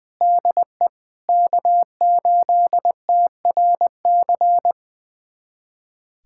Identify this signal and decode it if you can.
Morse code, heard as a keyed tone, about 20 wpm, 700 Hz. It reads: DE K8TRC
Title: USA Callsign DE Callsign - 5 Characters - 20wpm